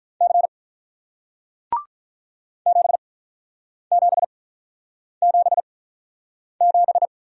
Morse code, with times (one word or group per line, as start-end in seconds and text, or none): 0.2-0.5 C
2.7-7.1 7777